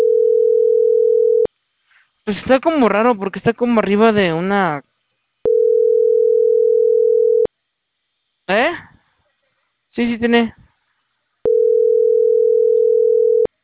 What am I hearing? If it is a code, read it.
Ringback tone (North America), 440 + 480 Hz, with speech.
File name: CENTRAL CAEA / out agcaea03